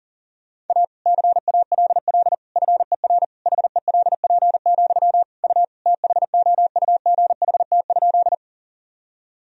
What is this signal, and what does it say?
Morse code, heard as a keyed tone, about 37 wpm, 700 Hz. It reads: A CALL FER HELP, U THOUGHT?
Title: Sentences from Top 300 Words and Abbreviations 40wpm